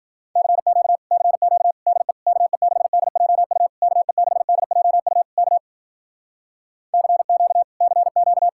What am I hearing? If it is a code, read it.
Morse code, heard as a keyed tone, about 50 wpm, 700 Hz. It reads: CQ CQ DE KE6DJU KE6DJU K CQ CQ